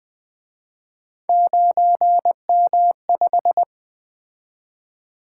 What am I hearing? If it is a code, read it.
Morse code, heard as a keyed tone, about 24 wpm, 700 Hz. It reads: 9M5
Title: Speed-Racing - Three Letter-Number Combinations - 20wpm